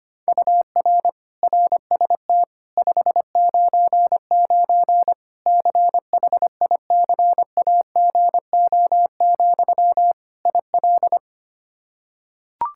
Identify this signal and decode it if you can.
Morse code, heard as a keyed tone, about 25 wpm, 700 Hz. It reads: UR RST 599 CHICAGO, IL
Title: Practice QSO - Exchange 1 - Beginning - 25wpm